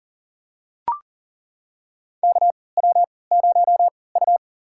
Morse code, heard as a keyed tone, about 40 wpm, spaced as for 23 wpm, 700 Hz. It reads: KW0U
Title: USA Call Signs - 4 Characters - 3x Character Spacing - 40wpm